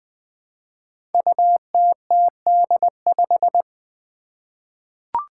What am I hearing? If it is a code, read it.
Morse code, heard as a keyed tone, about 21 wpm, 700 Hz. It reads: UTTD5
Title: Five Challenging Letter-Number Combinations 20wpm